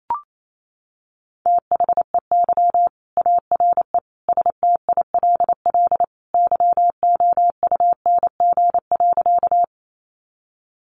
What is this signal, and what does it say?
Morse code, heard as a keyed tone, about 28 wpm, 700 Hz. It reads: THEY ARE STILL YOUNG.
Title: Sentences from Top 400 Words 28wpm